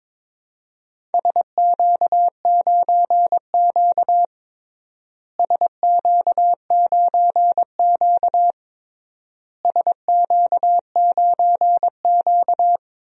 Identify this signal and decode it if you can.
Morse code, heard as a keyed tone, about 22 wpm, 700 Hz. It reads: SQ9Q SQ9Q SQ9Q
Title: DXCC-100 Call Signs - 4 Characters - Repeated 3x - 22wpm